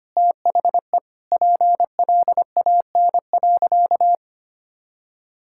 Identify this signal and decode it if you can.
Morse code, heard as a keyed tone, about 25 wpm, 700 Hz. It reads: THE PLAN.
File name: practice / 25wpm / Sentences with 2-Word Phrases - 25wpm